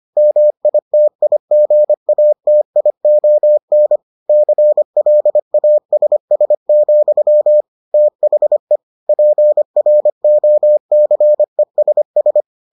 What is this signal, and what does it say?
Morse code, heard as a keyed tone, about 25 wpm, 600 Hz. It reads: MITIGATION CLASS, THE PROCESS